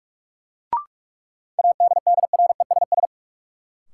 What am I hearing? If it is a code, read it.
Morse code, heard as a keyed tone, about 43 wpm, 700 Hz. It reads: ADDRESS
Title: Sentences from Top 1000 Words - New Word Drill - 45wpm